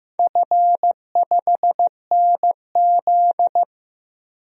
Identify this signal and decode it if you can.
Morse code, heard as a keyed tone, about 15 wpm, 700 Hz. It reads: F5NZ
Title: DXCC-100 Call Signs - 4 Characters - 15wpm